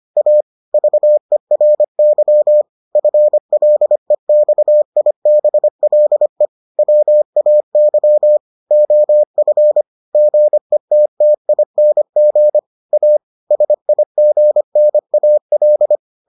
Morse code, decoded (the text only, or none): A VERY FLEXIBLE WAY OF GETTING A SIGNAL